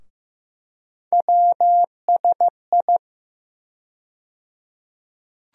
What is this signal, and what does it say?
Morse code, heard as a keyed tone, about 15 wpm, 700 Hz. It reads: WSI
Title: USA Call Sign Postfixes - 3 Letters 15wpm